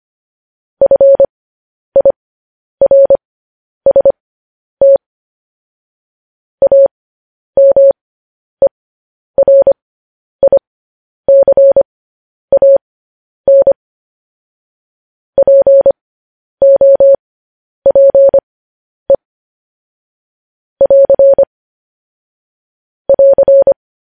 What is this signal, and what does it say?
Morse code, heard as a keyed tone, about 25 wpm, spaced as for 10 wpm, 550 Hz. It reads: FIRST AMERICAN POPE <AR> <AR>
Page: news-headlines-in-morse-code-at-10-wpm